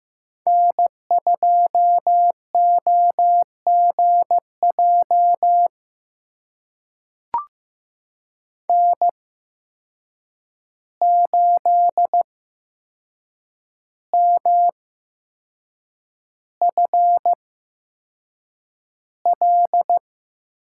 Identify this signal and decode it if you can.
Morse code, heard as a keyed tone, about 15 wpm, spaced as for 6 wpm, 700 Hz. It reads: N2OGJ N8MFL
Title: USA Call Signs - 5 Characters - 8x Character Spacing - 15wpm